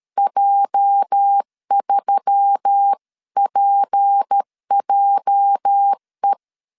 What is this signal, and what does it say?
Morse code, heard as a keyed tone, about 13 wpm, 800 Hz. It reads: J3PJE
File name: LEZIONI / 8 YJ45 / 8 YJ 12wpm 2235